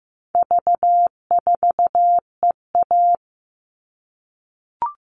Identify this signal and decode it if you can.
Morse code, heard as a keyed tone, about 15 wpm, 700 Hz. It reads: V4EA
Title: All Cut Numbers - 4-Digit 15wpm